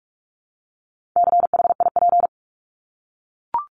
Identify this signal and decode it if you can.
Morse code, heard as a keyed tone, about 44 wpm, 700 Hz. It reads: CHIP
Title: Top 1751-2000 Words 45wpm